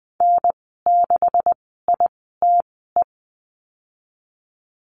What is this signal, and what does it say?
Morse code, heard as a keyed tone, about 19 wpm, spaced as for 14 wpm, 700 Hz. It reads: N6ITE
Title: 128 Lesson 21g - Number 6 - Call Signs 20wpm